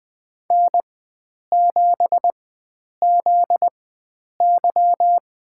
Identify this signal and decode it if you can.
Morse code, heard as a keyed tone, about 20 wpm, spaced as for 9 wpm, 700 Hz. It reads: N7ZY
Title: USA Call Signs - 4 Characters - 4x Character Spacing - 20wpm